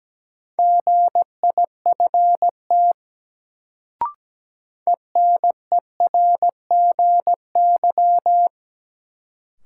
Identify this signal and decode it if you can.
Morse code, heard as a keyed tone, about 17 wpm, 700 Hz. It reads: GIFT ENERGY
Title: Sentences from Top 900 Words - New Word Drill - 17wpm